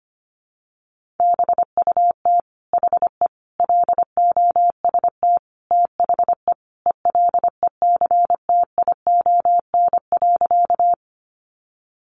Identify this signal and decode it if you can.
Morse code, heard as a keyed tone, about 25 wpm, 700 Hz. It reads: BUT HE LOST THE ELECTION.
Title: Sentences from Top 1500 Words 25wpm